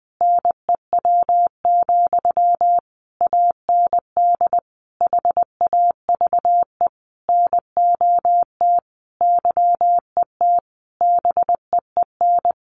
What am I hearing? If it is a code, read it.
Morse code, heard as a keyed tone, about 20 wpm, 700 Hz. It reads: NEW, AND HAVE NOT YET BEEN
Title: Sentences from Top 700 Words 20wpm